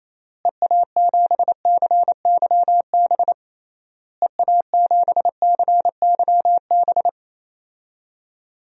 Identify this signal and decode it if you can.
Morse code, heard as a keyed tone, about 28 wpm, 700 Hz. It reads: EA7CYB EA7CYB
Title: DXCC-100 Call Signs - 6 Characters - Repeated 3x - 28wpm